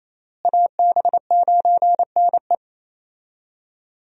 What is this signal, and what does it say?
Morse code, heard as a keyed tone, about 28 wpm, 700 Hz. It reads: AB9NE
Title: USA Call Signs - 5 Characters - Repeated 3x - 28wpm